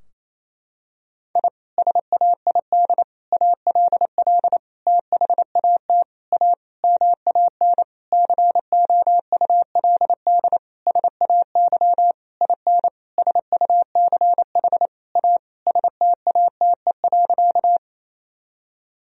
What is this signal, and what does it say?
Morse code, heard as a keyed tone, about 28 wpm, 700 Hz. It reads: I SAID ALL THAT A MAN COULD SAY IN SUCH A STATE.
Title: Sentences from Top 1000 Words - Review Part 2 - 28wpm